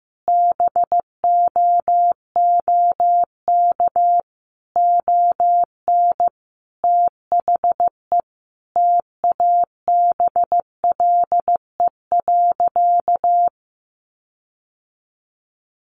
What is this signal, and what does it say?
Morse code, heard as a keyed tone, about 14 wpm, 700 Hz. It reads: BOOK ON THE TABLE.
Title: Sentences from Top 1400 Words 15wpm